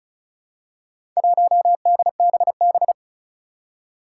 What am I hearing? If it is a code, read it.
Morse code, heard as a keyed tone, about 36 wpm, 700 Hz. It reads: 1DBB
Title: Four Challenging Letter-Number Combinations 35wpm